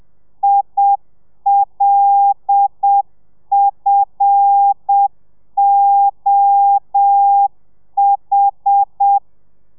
Morse code, heard as a keyed tone, about 7 wpm, 800 Hz. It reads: ILFOH
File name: telegrafia / i2tfj / Lez5 30